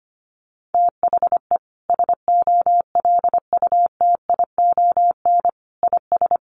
Morse code, heard as a keyed tone, about 25 wpm, 700 Hz. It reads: THE SOLUTION IS
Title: Sentences from Top 1300 Words 25wpm